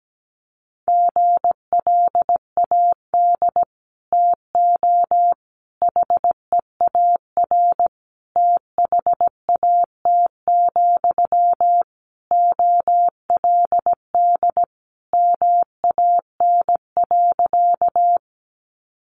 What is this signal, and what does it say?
Morse code, heard as a keyed tone, about 17 wpm, 700 Hz. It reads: GLAD TO HEAR THAT, OLD MAN.
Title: Sentences from Top 800 Words 17wpm